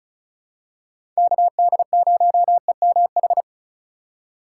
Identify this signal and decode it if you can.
Morse code, heard as a keyed tone, about 36 wpm, 700 Hz. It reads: KD0EMH
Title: USA Call Signs - 6 Characters - 2x Character Spacing - 35wpm